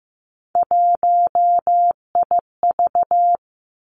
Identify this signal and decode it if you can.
Morse code, heard as a keyed tone, about 15 wpm, 700 Hz. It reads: 1IV
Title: Three Challenging Letter-Number Combinations 15wpm